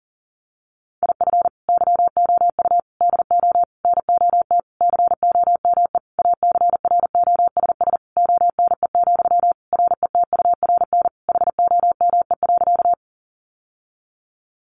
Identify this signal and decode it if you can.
Morse code, heard as a keyed tone, about 40 wpm, 700 Hz. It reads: IF YOU DO NOT COME ACROSS ONE, RETURN HOME.
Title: Sentences from Top 800 Words 40wpm